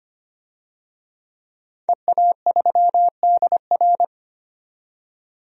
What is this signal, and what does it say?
Morse code, heard as a keyed tone, about 25 wpm, 700 Hz. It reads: EA3DR
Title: DXCC-100 Call Signs - 5 Characters - 25wpm